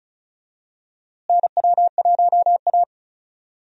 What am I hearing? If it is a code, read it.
Morse code, heard as a keyed tone, about 34 wpm, 700 Hz. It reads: NW1A
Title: USA Call Signs - 4 Characters - Repeated 3x - 35wpm